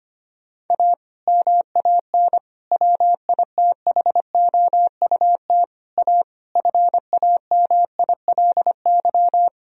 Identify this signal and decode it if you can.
Morse code, heard as a keyed tone, about 25 wpm, 700 Hz. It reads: A MAN WITHOUT A FAMILY
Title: Sentences from Top 1000 Words - Review Part 3 - 25wpm